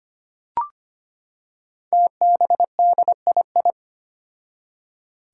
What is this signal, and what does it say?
Morse code, heard as a keyed tone, about 25 wpm, 700 Hz. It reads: TBDII
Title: Five Challenging Letter-Number Combinations 25wpm